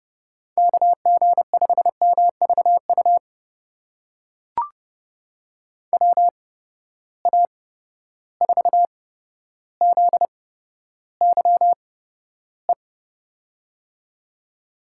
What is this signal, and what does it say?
Morse code, heard as a keyed tone, about 30 wpm, spaced as for 13 wpm, 700 Hz. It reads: KG5MVU WA4ZYE
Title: USA Call Signs - 6 Characters - 8x Character Spacing - 30wpm